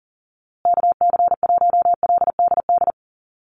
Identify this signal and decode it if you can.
Morse code, heard as a keyed tone, about 40 wpm, 700 Hz. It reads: KC1LDD
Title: USA Call Signs Repeated 2x 40wpm